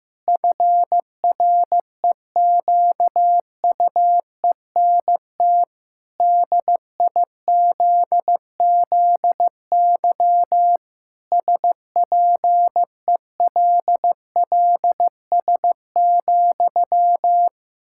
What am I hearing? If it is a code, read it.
Morse code, heard as a keyed tone, about 15 wpm, 700 Hz. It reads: FREQUENT DIZZY SPELLS,
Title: Pangram Challenge v2 - 15wpm